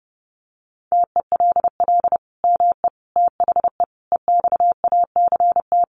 Morse code, heard as a keyed tone, about 30 wpm, 700 Hz. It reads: TELL ME THE EXACT
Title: Sentences from Top 1300 Words 30wpm